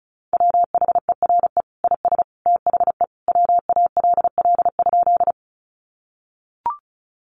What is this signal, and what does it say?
Morse code, heard as a keyed tone, about 34 wpm, 700 Hz. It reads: WHERE IS THE WALL?
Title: Sentences from Top 1100 Words 35wpm